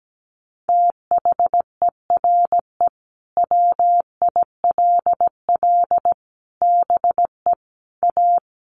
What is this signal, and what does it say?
Morse code, heard as a keyed tone, about 17 wpm, 700 Hz. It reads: THERE WILL BE A
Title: Sets of 4 Words 17wpm